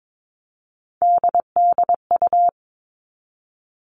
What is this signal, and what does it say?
Morse code, heard as a keyed tone, about 22 wpm, 700 Hz. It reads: DDU
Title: All Cut Numbers - 3-Digit 22wpm